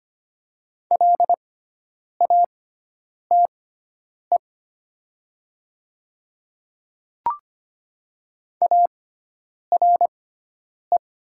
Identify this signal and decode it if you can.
Morse code, heard as a keyed tone, about 25 wpm, spaced as for 9 wpm, 700 Hz. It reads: LATE ARE
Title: 6x Character Spacing - Top 300 Words - 25wpm